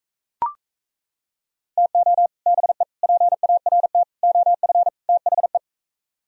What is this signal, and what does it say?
Morse code, heard as a keyed tone, about 35 wpm, 700 Hz. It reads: TO BE PART OF THE
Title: Speed-Racing - Sets of 5 Words 28wpm